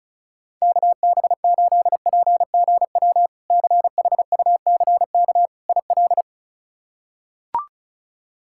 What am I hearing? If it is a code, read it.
Morse code, heard as a keyed tone, about 36 wpm, 700 Hz. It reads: KB8PGW CHUCK IL
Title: K1USN SST QSO Practice - 35wpm